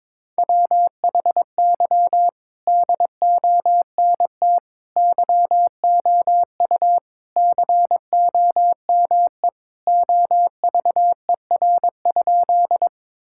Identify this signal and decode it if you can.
Morse code, heard as a keyed tone, about 22 wpm, 700 Hz. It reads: WHY DONT YOU COME OVER?
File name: practice / 22wpm / Sentences with 4-Word Phrases - 22wpm